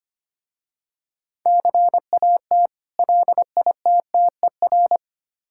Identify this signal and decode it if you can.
Morse code, heard as a keyed tone, about 25 wpm, 700 Hz. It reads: CAT LITTER
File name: practice / 25wpm / Difficult Words with Hints - 25wpm